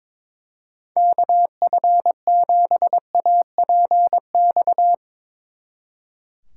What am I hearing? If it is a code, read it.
Morse code, heard as a keyed tone, about 24 wpm, 700 Hz. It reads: KF7APX
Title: Speed Racing - USA 2x3 Call Signs 22wpm